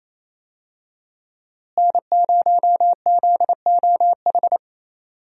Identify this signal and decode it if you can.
Morse code, heard as a keyed tone, about 28 wpm, 700 Hz. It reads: N0ZOH
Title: USA Call Signs - 5 Characters - Repeated 2x - 28wpm